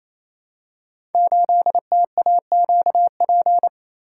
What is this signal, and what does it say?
Morse code, heard as a keyed tone, about 28 wpm, 700 Hz. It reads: 8TAQP
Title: Code Groups - 5 Letter:Numbers - 28wpm